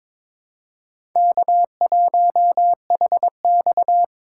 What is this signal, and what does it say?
Morse code, heard as a keyed tone, about 22 wpm, 700 Hz. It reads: K1HX
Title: USA Call Signs - 4 Characters - 2x Character Spacing - 22wpm